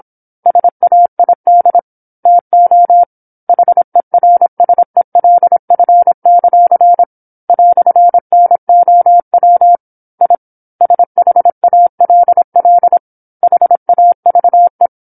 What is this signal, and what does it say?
Morse code, heard as a keyed tone, about 27 wpm, 700 Hz. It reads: SAID TO HERSELF; "NOW I SHALL HAVE
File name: 08-Chapter-08 26 WPM